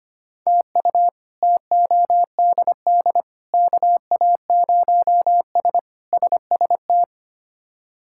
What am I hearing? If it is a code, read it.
Morse code, heard as a keyed tone, about 25 wpm, 700 Hz. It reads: TU TODD KA0S SST
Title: K1USN SST QSO Practice - 25wpm